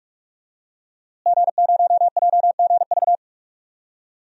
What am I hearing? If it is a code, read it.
Morse code, heard as a keyed tone, about 46 wpm, 700 Hz. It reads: G0JGV
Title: DXCC-100 Call Signs - 5 Characters - Repeated 3x - 45wpm